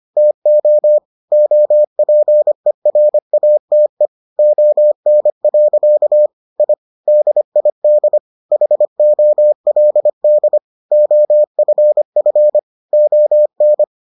Morse code, heard as a keyed tone, about 25 wpm, 600 Hz. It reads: TO OPERATE ON. I DID HOLD OFF ON